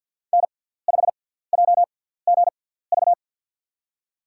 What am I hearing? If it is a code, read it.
Morse code, heard as a keyed tone, about 46 wpm, spaced as for 17 wpm, 700 Hz. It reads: N5JGV
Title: USA Call Signs - 5 Characters - 6x Character Spacing - 50wpm